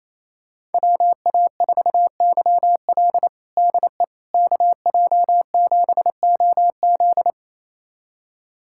Morse code, heard as a keyed tone, about 28 wpm, 700 Hz. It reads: WA4YL DE KJ7OZ
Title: USA Callsign DE Callsign 28wpm